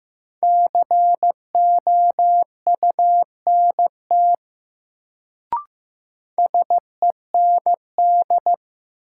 Morse code, heard as a keyed tone, about 15 wpm, 700 Hz. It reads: COUNT SEND
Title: Sentences from Top 1500 Words - New Word Drill - 15wpm